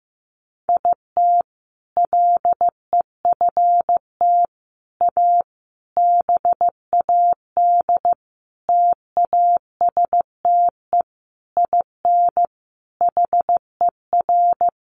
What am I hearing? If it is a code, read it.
Morse code, heard as a keyed tone, about 15 wpm, 700 Hz. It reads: IT LEFT A BAD TASTE IN HER
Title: Sentences from Top 1500 Words 15wpm